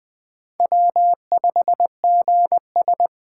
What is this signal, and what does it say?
Morse code, heard as a keyed tone, about 21 wpm, 700 Hz. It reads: W5GS